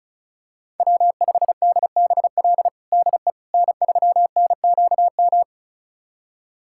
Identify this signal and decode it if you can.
Morse code, heard as a keyed tone, about 34 wpm, 700 Hz. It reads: W5DBL DE N3NQM
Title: USA Callsign DE Callsign - 5 Characters - 35wpm